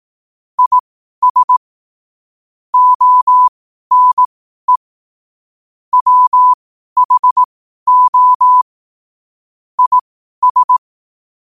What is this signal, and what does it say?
Morse code, heard as a keyed tone, about 18 wpm, spaced as for 12 wpm, 1000 Hz. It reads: IS ONE WHO IS
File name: qotdfiles / 12wpm